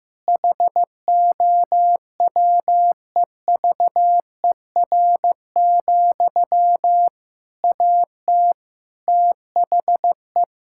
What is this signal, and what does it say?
Morse code, heard as a keyed tone, about 15 wpm, 700 Hz. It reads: HOWEVER, AT THE
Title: Sentences from Top 1000 Words - Review Part 5 - 15wpm